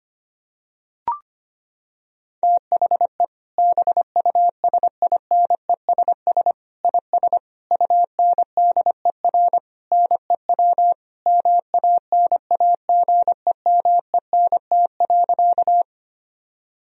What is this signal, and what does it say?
Morse code, heard as a keyed tone, about 25 wpm, 700 Hz. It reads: THE BUSINESS IS UNDER NEW MANAGEMENT.
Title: Sentences from Top 1000 Words - Review Part 4 - 25wpm